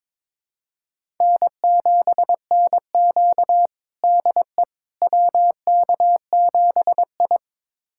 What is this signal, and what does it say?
Morse code, heard as a keyed tone, about 22 wpm, 700 Hz. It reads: N7NQ DE WK7I
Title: USA Callsign DE Callsign - 4 Characters - 22wpm